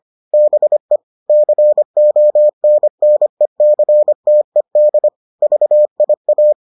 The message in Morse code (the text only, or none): BE CONNECTED VIA